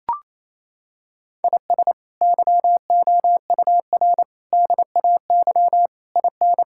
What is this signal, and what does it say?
Morse code, heard as a keyed tone, about 28 wpm, 700 Hz. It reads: IS YOUR DAY IN
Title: Sentences from Top 1000 Words - Review Part 6 - 28wpm